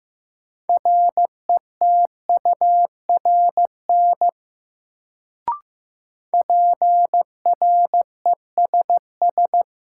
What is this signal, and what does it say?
Morse code, heard as a keyed tone, about 15 wpm, 700 Hz. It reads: RETURN PRESS
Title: Sentences from Top 800 Words - New Word Drill - 15wpm